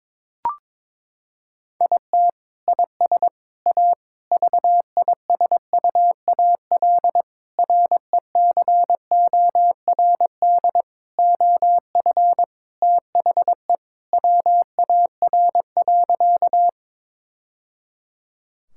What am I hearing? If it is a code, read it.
Morse code, heard as a keyed tone, about 22 wpm, 700 Hz. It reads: IT IS A VISUAL RECORD OF THE WAR.
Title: Sentences from Top 1000 Words - Review Part 6 - 22wpm